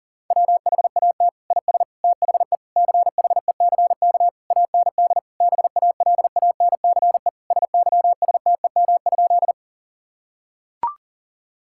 Morse code, heard as a keyed tone, about 40 wpm, 700 Hz. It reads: WHAT IS THE CHECK AND BALANCE SYSTEM?
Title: Sentences from Top 1000 Words - Review Part 6 - 40wpm